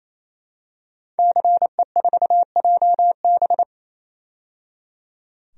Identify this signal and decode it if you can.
Morse code, heard as a keyed tone, about 27 wpm, 700 Hz. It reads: CE4JB
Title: DXCC-100 Call Signs - 5 Characters - 28wpm